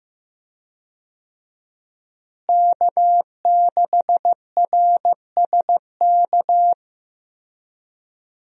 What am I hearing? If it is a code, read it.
Morse code, heard as a keyed tone, about 15 wpm, 700 Hz. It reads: K6RSK
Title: USA Call Signs - 5 Characters - Repeated 2x - 15wpm